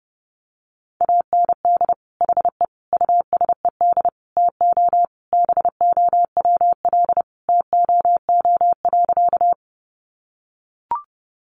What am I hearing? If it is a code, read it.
Morse code, heard as a keyed tone, about 30 wpm, 700 Hz. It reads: AND HE USED TO BOWL TOO.
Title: Sentences from Top 1200 Words 30wpm